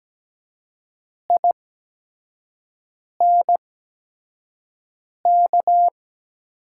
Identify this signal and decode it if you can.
Morse code, heard as a keyed tone, about 17 wpm, spaced as for 5 wpm, 700 Hz. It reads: INK
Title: 8x Character Spacing - Top 300 Words - 17wpm